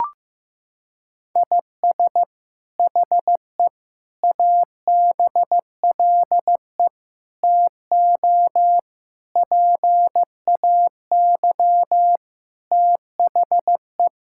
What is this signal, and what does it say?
Morse code, heard as a keyed tone, about 15 wpm, 700 Hz. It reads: IS HE ABLE TO PAY THE
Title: Sentences from Top 1000 Words - Review Part 5 - 15wpm